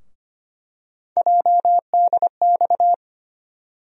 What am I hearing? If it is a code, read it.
Morse code, heard as a keyed tone, about 25 wpm, 700 Hz. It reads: JDX
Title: USA Call Sign Postfixes - 3 Letters 25wpm